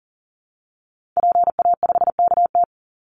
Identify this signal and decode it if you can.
Morse code, heard as a keyed tone, about 40 wpm, 700 Hz. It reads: PA5KT
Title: Call Signs Repeated 3x - 40wpm